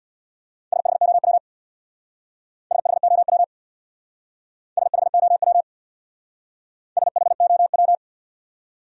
Morse code, heard as a keyed tone, about 64 wpm, 700 Hz. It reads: SHOW SHOW SHOW SHOW